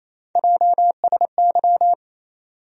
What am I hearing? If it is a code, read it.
Morse code, heard as a keyed tone, about 28 wpm, 700 Hz. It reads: JSY
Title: USA Call Sign Postfixes - 3 Letters 28wpm